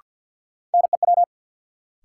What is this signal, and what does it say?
Morse code, heard as a keyed tone, about 53 wpm, 700 Hz. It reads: NEW
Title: Sets of 2 Words - New Word Drill - 50wpm